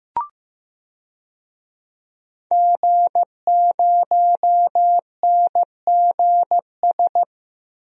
Morse code, heard as a keyed tone, about 15 wpm, 700 Hz. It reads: G0NGS
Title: DXCC-100 Call Signs - 5 Characters - 15wpm